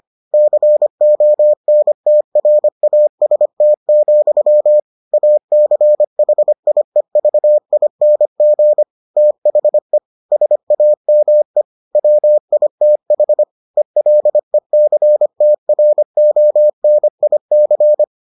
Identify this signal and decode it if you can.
Morse code, heard as a keyed tone, about 25 wpm, 600 Hz. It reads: CONTRAST, ACHIEVING THE SAME WITH ELECTRONIC